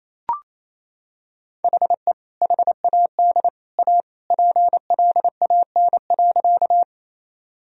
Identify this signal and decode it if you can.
Morse code, heard as a keyed tone, about 28 wpm, 700 Hz. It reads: HE HAD A PLAN.
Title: Sentences from Top 1000 Words - Review Part 2 - 28wpm